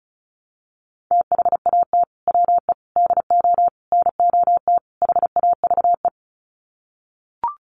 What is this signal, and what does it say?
Morse code, heard as a keyed tone, about 34 wpm, 700 Hz. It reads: THAT WE DO NOT HAVE
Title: Sets of 5 Words Encore 35wpm